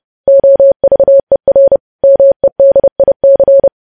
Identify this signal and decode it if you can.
Morse code, heard as a keyed tone, about 30 wpm, 550 Hz. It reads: OVER MEDIC
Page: news headlines in morse code at 30 wpm